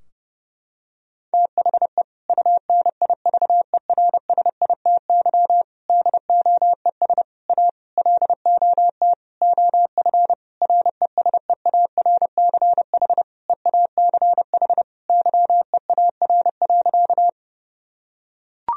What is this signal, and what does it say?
Morse code, heard as a keyed tone, about 30 wpm, 700 Hz. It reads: THE UNIVERSITY DOES A LOT OF RESEARCH EACH YEAR.
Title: Sentences from Top 1000 Words - Review Part 4 - 30wpm